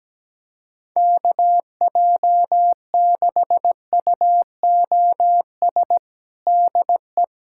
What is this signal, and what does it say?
Morse code, heard as a keyed tone, about 17 wpm, 700 Hz. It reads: KJ6UOS DE
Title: USA Callsign DE Callsign - 6 Characters - 17wpm